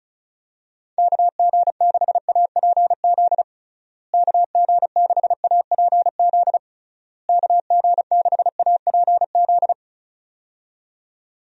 Morse code, heard as a keyed tone, about 36 wpm, 700 Hz. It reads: KG6APZ KG6APZ KG6APZ
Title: USA Call Signs - 6 Characters - Repeated 3x - 35wpm